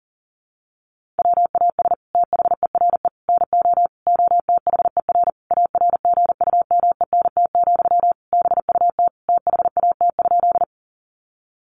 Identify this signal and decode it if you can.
Morse code, heard as a keyed tone, about 40 wpm, 700 Hz. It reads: WAS THERE NO OTHER ARGUMENT, BUT THAT?
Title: Sentences from Top 1400 Words 40wpm